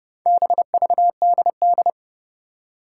Morse code, heard as a keyed tone, about 30 wpm, 690 Hz. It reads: BVDD
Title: Four Challenging Letter-Number Combinations 30wpm